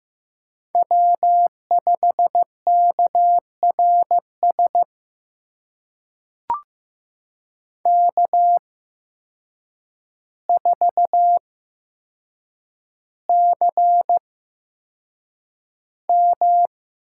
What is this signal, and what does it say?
Morse code, heard as a keyed tone, about 15 wpm, spaced as for 7 wpm, 700 Hz. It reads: W5KRS K4CM
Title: USA Call Signs - 5 Characters - 8x Character Spacing - 15wpm